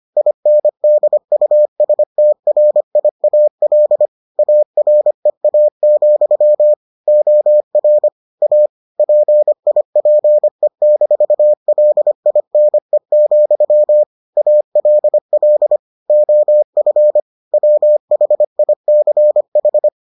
Morse code, heard as a keyed tone, about 25 wpm, 600 Hz. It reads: INDUSTRIAL AREA, OR A PIPE-LINE, ALL OF WHICH